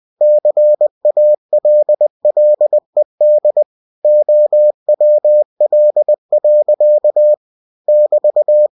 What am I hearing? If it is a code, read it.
Morse code, heard as a keyed tone, about 20 wpm, 600 Hz. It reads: CALLED OWL. <BT>
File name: cw-books / winnie-the-pooh / winnie-20wpm-page-0080